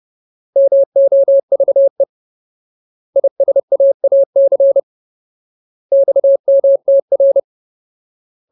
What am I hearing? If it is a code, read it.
Morse code, heard as a keyed tone, about 30 wpm, 550 Hz. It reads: MOVE ISAAC XMTR
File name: Ultimate Word List / 30WPM Ultimate